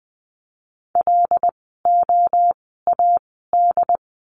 Morse code, heard as a keyed tone, about 20 wpm, spaced as for 14 wpm, 700 Hz. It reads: LOAD
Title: 075 Lesson 13f - Letter L - Words 20wpm